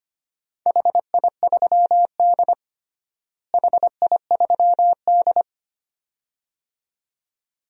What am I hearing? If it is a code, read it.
Morse code, heard as a keyed tone, about 25 wpm, 700 Hz. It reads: HI3D HI3D
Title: DXCC-100 Call Signs - 4 Characters - Repeated 2x - 25wpm